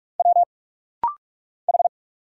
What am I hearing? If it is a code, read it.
Morse code, heard as a keyed tone, about 46 wpm, 700 Hz. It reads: W H
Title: Single Letter-Number Alt - Rapid-Fire 45wpm